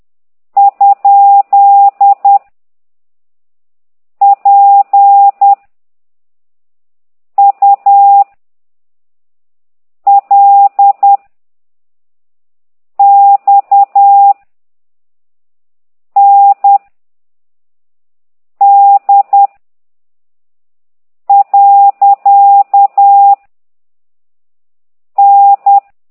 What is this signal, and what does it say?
Morse code, heard as a keyed tone, about 10 wpm, 800 Hz. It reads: ? P U L X N D . N
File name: ct2idw / exerc9